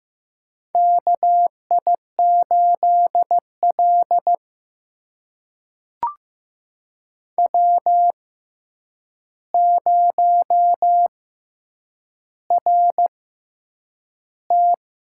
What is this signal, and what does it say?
Morse code, heard as a keyed tone, about 15 wpm, spaced as for 8 wpm, 700 Hz. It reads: KI8L W0RT
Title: USA Call Signs - 4 Characters - 6x Character Spacing - 15wpm